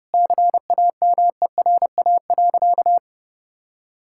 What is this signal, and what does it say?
Morse code, heard as a keyed tone, about 32 wpm, 700 Hz. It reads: CAMERA.
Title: Sentences from Top 1000 Words - Review Part 5 - 30wpm